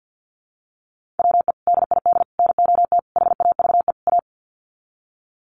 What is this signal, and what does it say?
Morse code, heard as a keyed tone, about 48 wpm, 700 Hz. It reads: WE DID NOT HAVE A
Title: Sets of 5 Words Encore2 50wpm